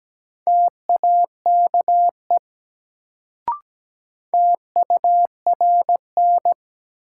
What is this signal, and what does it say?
Morse code, heard as a keyed tone, about 17 wpm, 700 Hz. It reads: TAKE TURN
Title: Sentences from Top 200 Words - New Word Drill - 17wpm